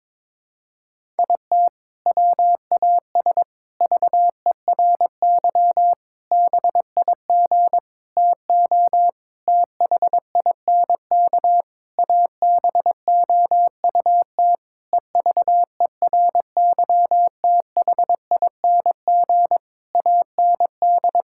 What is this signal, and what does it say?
Morse code, heard as a keyed tone, about 22 wpm, 700 Hz. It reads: IT WAS VERY BIG TO THINK ABOUT EVERYTHING AND
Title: Sentences from Top 1600 Words 22wpm